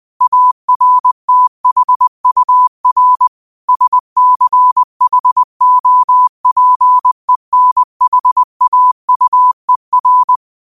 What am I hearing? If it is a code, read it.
Morse code, heard as a keyed tone, about 20 wpm, 1000 Hz. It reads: ARTHUR SCHOPENHAUER